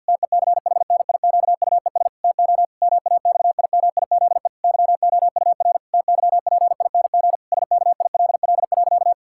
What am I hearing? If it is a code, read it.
Morse code, heard as a keyed tone, about 50 wpm, 700 Hz. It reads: TECHNIQUES TO MAXIMIZE YOUR TYPING SKILL.
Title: Pangram Challenge v2 - 50wpm